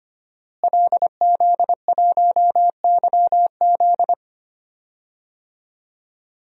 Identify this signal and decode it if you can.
Morse code, heard as a keyed tone, about 25 wpm, 700 Hz. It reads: LZ1YZ
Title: DXCC-100 Call Signs - 5 Characters - Repeated 2x - 25wpm